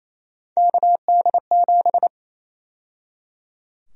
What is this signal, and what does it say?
Morse code, heard as a keyed tone, about 28 wpm, 700 Hz. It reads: KD7
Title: USA Call Sign Prefixes - 28wpm